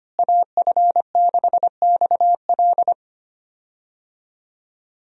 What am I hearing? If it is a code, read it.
Morse code, heard as a keyed tone, about 25 wpm, 700 Hz. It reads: AF6XL
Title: USA Call Signs - 5 Characters - Repeated 3x - 25wpm